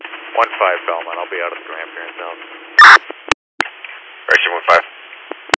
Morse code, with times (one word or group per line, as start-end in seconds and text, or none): none